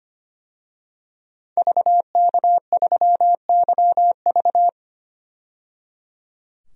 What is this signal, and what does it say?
Morse code, heard as a keyed tone, about 25 wpm, 690 Hz. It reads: VK3YV
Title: DXCC-100 Call Signs - 5 Characters - 25wpm